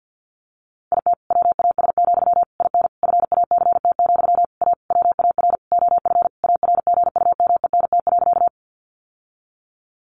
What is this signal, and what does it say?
Morse code, heard as a keyed tone, about 50 wpm, 700 Hz. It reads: IT WAS, IN FACT, A WAR OF ARGUMENT.
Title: Sentences from Top 1400 Words 50wpm